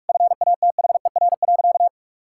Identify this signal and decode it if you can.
Morse code, heard as a keyed tone, about 46 wpm, 700 Hz. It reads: FATHER.
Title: Sentences from Top 1000 Words - Review Part 6 - 45wpm